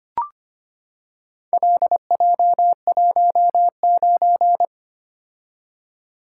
Morse code, heard as a keyed tone, about 25 wpm, 700 Hz. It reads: LJ19
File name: practice / 25wpm / Four Letter-Number Combinations - 25wpm